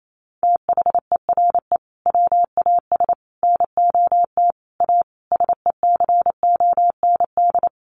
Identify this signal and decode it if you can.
Morse code, heard as a keyed tone, about 28 wpm, 700 Hz. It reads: THERE WAS NOT A SECOND
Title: Sentences from Top 1100 Words 28wpm